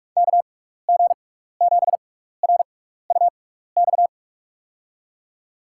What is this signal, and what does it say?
Morse code, heard as a keyed tone, about 46 wpm, spaced as for 16 wpm, 700 Hz. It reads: KG7RUX
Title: USA Call Signs - 6 Characters - 6x Character Spacing - 45wpm